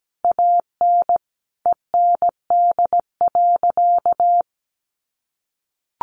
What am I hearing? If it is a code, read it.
Morse code, heard as a keyed tone, about 17 wpm, 700 Hz. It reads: AN END.
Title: Sentences from Top 1500 Words 17wpm